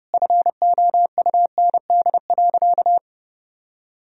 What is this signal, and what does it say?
Morse code, heard as a keyed tone, about 30 wpm, 700 Hz. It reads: FOUND.
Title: Sentences from Top 1000 Words - Review Part 4 - 30wpm